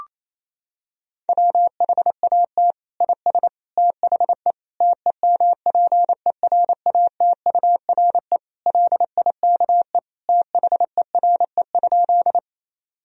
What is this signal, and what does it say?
Morse code, heard as a keyed tone, about 28 wpm, 700 Hz. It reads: WHAT IS THE TEMPERATURE LIKE THERE?
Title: Sentences from Top 1000 Words - Review Part 4 - 28wpm